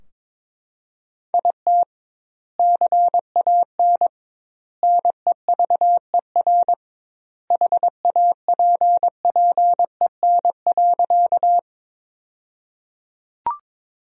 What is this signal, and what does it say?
Morse code, heard as a keyed tone, about 22 wpm, 700 Hz. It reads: IT CAN NEVER HAPPEN.
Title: 2x Word-Spacing - Sentences from Top 400 Words - 22wpm